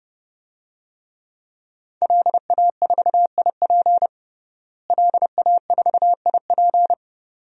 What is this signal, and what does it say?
Morse code, heard as a keyed tone, about 32 wpm, 700 Hz. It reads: LA4IP LA4IP
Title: DXCC-100 Call Signs - 5 Characters - Repeated 2x - 30wpm